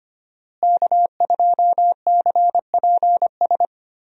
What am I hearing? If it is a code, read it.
Morse code, heard as a keyed tone, about 25 wpm, 700 Hz. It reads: K2CPS
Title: USA Call Signs - 5 Characters - Repeated 3x - 25wpm